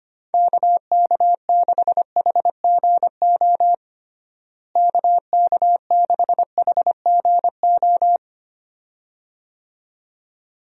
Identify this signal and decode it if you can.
Morse code, heard as a keyed tone, about 25 wpm, 700 Hz. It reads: KK6HGO KK6HGO
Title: USA Call Signs - 6 Characters - Repeated 2x - 25wpm